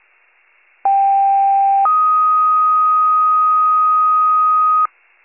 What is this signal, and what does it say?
Two-tone page, heard as a one-second tone, then a three-second tone, about 790 then 1250 Hz. It is heard from 0.8 to 4.9 s.